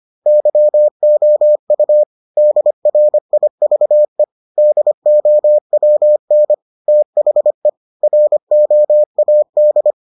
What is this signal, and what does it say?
Morse code, heard as a keyed tone, about 25 wpm, 600 Hz. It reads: YOU DRIVE DOWN THE ROAD